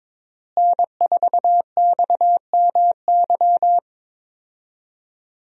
Morse code, heard as a keyed tone, about 22 wpm, 700 Hz. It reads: N4XMY
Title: USA Call Signs - 5 Characters - Repeated 3x - 22wpm